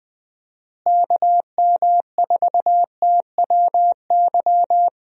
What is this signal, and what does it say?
Morse code, heard as a keyed tone, about 20 wpm, 700 Hz. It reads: KM4TWY
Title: USA Call Signs - 6 Characters - Repeated 2x - 20wpm